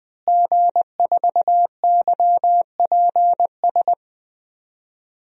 Morse code, heard as a keyed tone, about 21 wpm, 700 Hz. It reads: G4YPS
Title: DXCC-100 Call Signs - 5 Characters - Repeated 2x - 20wpm